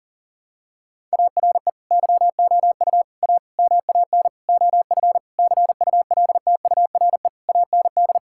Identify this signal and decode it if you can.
Morse code, heard as a keyed tone, about 40 wpm, 700 Hz. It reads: ARE YOU A MAN OF CULTURE AND
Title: Sentences from Top 1000 Words - Review Part 6 - 40wpm